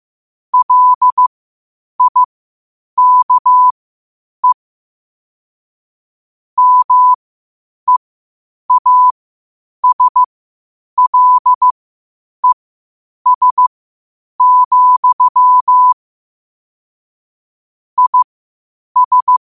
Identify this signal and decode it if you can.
Morse code, heard as a keyed tone, about 15 wpm, spaced as for 8 wpm, 1000 Hz. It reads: LIKE MEASLES, IS
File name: qotdfiles / 8wpm